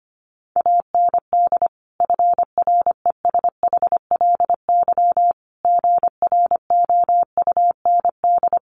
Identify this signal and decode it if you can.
Morse code, heard as a keyed tone, about 25 wpm, 700 Hz. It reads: AND FRESHLY GROUND
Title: Sets of 5 Words Encore 25wpm